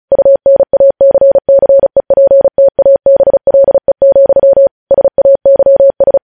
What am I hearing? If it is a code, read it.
Morse code, heard as a keyed tone, about 36 wpm, 550 Hz. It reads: UNACCEPTABLE, SAYS
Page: News Headlines in Morse Code at 35 WPM